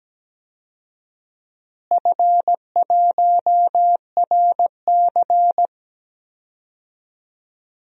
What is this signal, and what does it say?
Morse code, heard as a keyed tone, about 17 wpm, 700 Hz. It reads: F1RC
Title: DXCC-100 Call Signs - 4 Characters - Repeated 3x - 17wpm